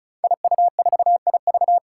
Morse code, heard as a keyed tone, about 36 wpm, 700 Hz. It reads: IU4IV
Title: Five Challenging Letter-Number Combinations 35wpm